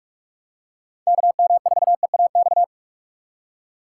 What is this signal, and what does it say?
Morse code, heard as a keyed tone, about 46 wpm, 700 Hz. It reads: KM4EAX
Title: USA Call Signs - 6 Characters - 4x Character Spacing - 45wpm